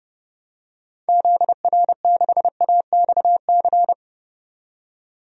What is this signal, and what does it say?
Morse code, heard as a keyed tone, about 32 wpm, 700 Hz. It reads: ZR6AXC
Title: DXCC-100 Call Signs - 6 Characters - 30wpm